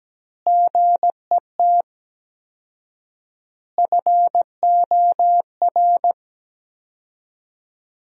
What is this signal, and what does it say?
Morse code, heard as a keyed tone, about 17 wpm, 700 Hz. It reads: GET FOR
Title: 4x Word-Spacing - Sentences from Top 400 Words - 17wpm